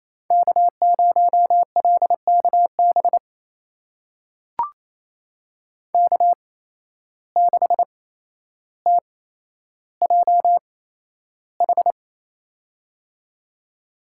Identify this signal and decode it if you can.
Morse code, heard as a keyed tone, about 27 wpm, spaced as for 12 wpm, 700 Hz. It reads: K0LKB K6TJH
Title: USA Call Signs - 8x Character Spacing - 28wpm